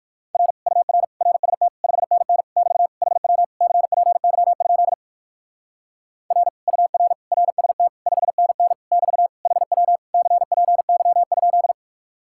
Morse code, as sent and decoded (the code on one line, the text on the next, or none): .-. / ..- .-. / .-. ... - / ..... -. -. / -...- / .... .-- / -.-. .--. -.-- ..--.. / .-. / ..- .-. / .-. ... - / ..... -. -. / -...- / .... .-- / -.-. .--. -.-- ..--..
R UR RST 5NN <BT> HW CPY? R UR RST 5NN <BT> HW CPY?